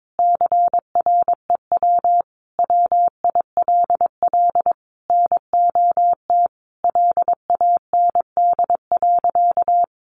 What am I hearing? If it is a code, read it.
Morse code, heard as a keyed tone, about 22 wpm, 700 Hz. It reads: CREW WILL NOT LAND.
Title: Sentences from Top 1500 Words 22wpm